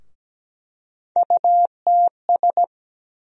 Morse code, heard as a keyed tone, about 17 wpm, 700 Hz. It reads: UTS
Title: USA Call Sign Postfixes - 3 Letters 17wpm